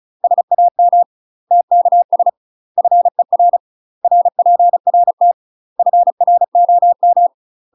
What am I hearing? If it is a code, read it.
Morse code, heard as a keyed tone, about 36 wpm, 700 Hz. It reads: SAM TKS FER RPRT FROM
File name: UK MASTER QSOs TRAINING / 35WPM / UK MASTER QSOs 0026